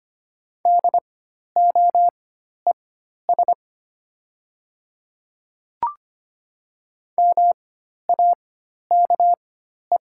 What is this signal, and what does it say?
Morse code, heard as a keyed tone, about 25 wpm, spaced as for 12 wpm, 700 Hz. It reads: DOES MAKE
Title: 4x Character Spacing - Top 200 Words - 25wpm